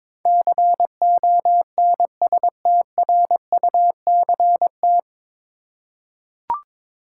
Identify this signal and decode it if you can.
Morse code, heard as a keyed tone, about 22 wpm, 700 Hz. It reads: CONSTRUCT
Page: Morse Code Ninja practice 22wpm